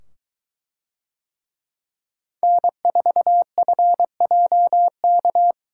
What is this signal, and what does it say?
Morse code, heard as a keyed tone, about 22 wpm, 700 Hz. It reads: N4FJK